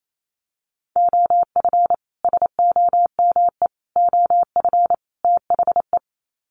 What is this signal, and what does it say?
Morse code, heard as a keyed tone, about 28 wpm, 700 Hz. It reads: OF SOME OF THE
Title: Sets of 4 Words 28wpm